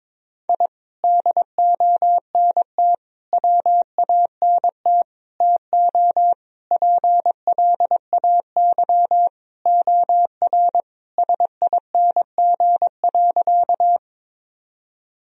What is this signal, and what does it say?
Morse code, heard as a keyed tone, about 22 wpm, 700 Hz. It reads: I DONT WANT TO PLAY OR SING.
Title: Sentences from Top 1000 Words - Review Part 2 - 22wpm